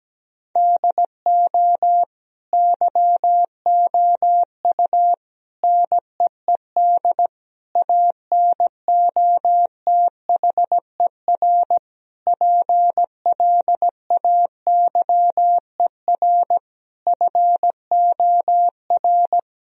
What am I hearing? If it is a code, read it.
Morse code, heard as a keyed tone, about 17 wpm, 700 Hz. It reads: DO YOU NEED ANOTHER PLAYER FOR
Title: Sentences from Top 1000 Words - Review Part 4 - 17wpm